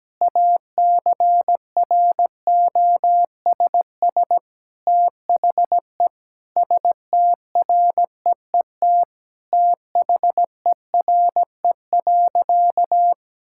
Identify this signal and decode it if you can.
Morse code, heard as a keyed tone, about 17 wpm, 700 Hz. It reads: ACROSS THE STREET THERE.
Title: Sentences from Top 1000 Words - Review Part 5 - 17wpm